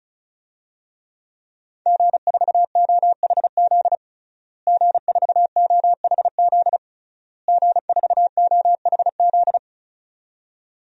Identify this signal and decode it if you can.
Morse code, heard as a keyed tone, about 36 wpm, 700 Hz. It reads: G4OHZ G4OHZ G4OHZ
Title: DXCC-100 Call Signs - 5 Characters - Repeated 3x - 35wpm